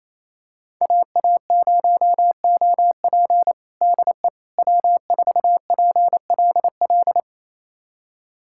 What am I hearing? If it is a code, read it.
Morse code, heard as a keyed tone, about 28 wpm, 690 Hz. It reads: AA0OP DE W4PLL
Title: USA Callsign DE Callsign - 5 Characters - 28wpm